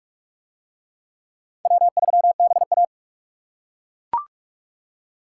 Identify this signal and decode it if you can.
Morse code, heard as a keyed tone, about 38 wpm, 700 Hz. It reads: W3BA